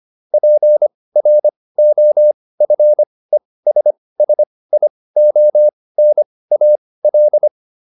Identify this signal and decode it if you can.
Morse code, heard as a keyed tone, about 25 wpm, spaced as for 18 wpm, 600 Hz. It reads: PROFESSIONAL